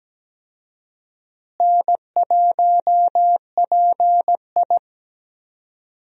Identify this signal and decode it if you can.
Morse code, heard as a keyed tone, about 17 wpm, 700 Hz. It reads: N1PI
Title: USA Call Signs - 4 Characters - Repeated 2x - 17wpm